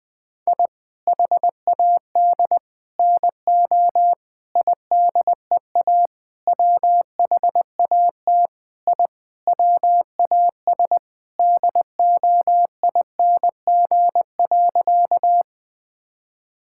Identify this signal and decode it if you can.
Morse code, heard as a keyed tone, about 21 wpm, 700 Hz. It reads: I HAD NO IDEA WHAT I WAS DOING.
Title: Sentences with 8-Word Phrases - Spoken Beginning - 20wpm